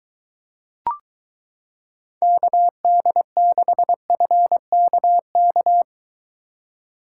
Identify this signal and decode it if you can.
Morse code, heard as a keyed tone, about 23 wpm, 700 Hz. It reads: KD6FKK